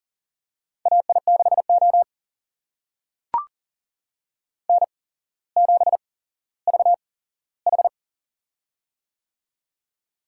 Morse code, heard as a keyed tone, about 40 wpm, spaced as for 17 wpm, 700 Hz. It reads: AI6O N7VH
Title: USA Call Signs - 4 Characters - 8x Character Spacing - 40wpm